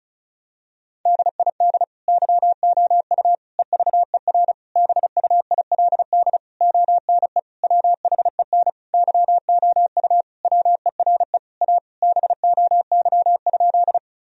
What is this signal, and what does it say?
Morse code, heard as a keyed tone, about 36 wpm, 700 Hz. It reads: DID YOU EVER BUILD ONE WHEN YOU WERE A BOY?
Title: Sentences from Top 1000 Words - Review Part 1 - 35wpm